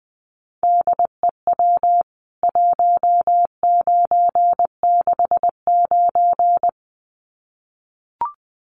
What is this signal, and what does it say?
Morse code, heard as a keyed tone, about 20 wpm, 700 Hz. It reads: DEW 1969
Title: CWT Exchange 20wpm